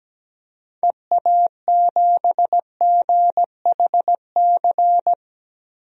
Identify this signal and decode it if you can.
Morse code, heard as a keyed tone, about 17 wpm, 700 Hz. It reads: EA7GHC
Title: DXCC-100 Call Signs - 6 Characters - 17wpm